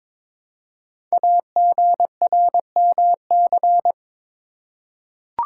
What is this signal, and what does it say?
Morse code, heard as a keyed tone, about 22 wpm, 700 Hz. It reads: AGRMC
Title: Code Groups - 5 Letters - 22wpm